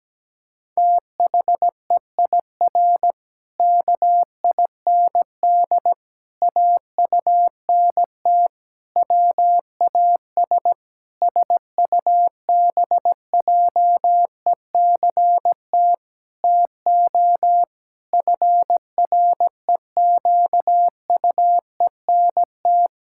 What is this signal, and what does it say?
Morse code, heard as a keyed tone, about 17 wpm, 700 Hz. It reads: THEIR KIND AUNT WAS SUBJECT TO FREQUENT
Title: Pangram Challenge v2 - 17wpm